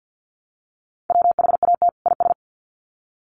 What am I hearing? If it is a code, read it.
Morse code, heard as a keyed tone, about 48 wpm, 700 Hz. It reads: WHAT IS
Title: Sets of 2 Words 50wpm